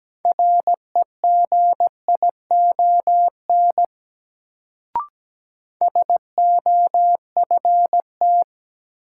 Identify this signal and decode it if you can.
Morse code, heard as a keyed tone, about 17 wpm, 700 Hz. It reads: REGION SOFT
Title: Sentences from Top 1500 Words - New Word Drill - 17wpm